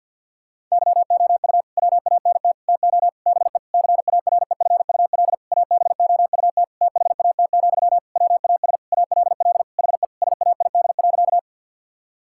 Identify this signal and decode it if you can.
Morse code, heard as a keyed tone, about 52 wpm, 700 Hz. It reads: YOU WANT TO BE CAREFUL ABOUT THAT, WAS ALL HE SAID.
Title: Sentences from Top 1600 Words 50wpm